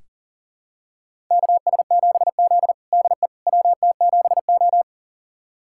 Morse code, heard as a keyed tone, about 37 wpm, 700 Hz. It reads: KS7Z DE WT7O
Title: USA Callsign DE Callsign - 4 Characters - 40wpm